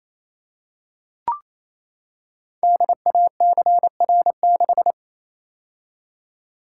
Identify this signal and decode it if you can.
Morse code, heard as a keyed tone, about 27 wpm, 700 Hz. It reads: DACR6
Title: Code Groups - 5 Letter:Numbers - 28wpm